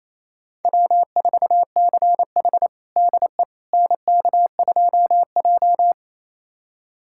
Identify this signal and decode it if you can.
Morse code, heard as a keyed tone, about 27 wpm, 700 Hz. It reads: W4CH DE NK2J
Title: USA Callsign DE Callsign - 4 Characters - 28wpm